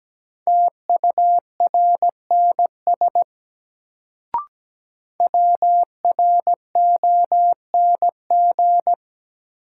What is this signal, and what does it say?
Morse code, heard as a keyed tone, about 17 wpm, 700 Hz. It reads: TURNS WRONG
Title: Sets of 5 Words Encore - New Word Drill - 17wpm